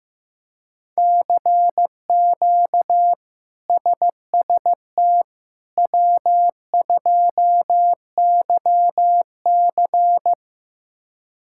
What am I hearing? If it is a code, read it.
Morse code, heard as a keyed tone, about 15 wpm, 700 Hz. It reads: CQ SST W2YC
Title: K1USN SST QSO Practice - 15wpm